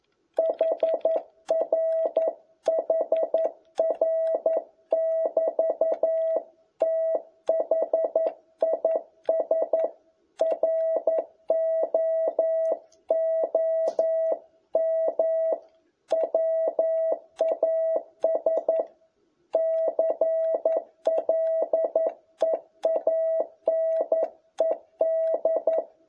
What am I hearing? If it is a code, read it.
Morse code, heard as a keyed tone, about 11 wpm, 660 Hz. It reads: HR HR<BT> THIS ROOM WAS CLEANED
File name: jh8cbh / 2021cw